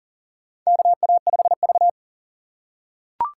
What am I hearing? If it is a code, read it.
Morse code, heard as a keyed tone, about 40 wpm, 700 Hz. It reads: KA5V
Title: USA Call Signs - 3x Character Spacing - 40wpm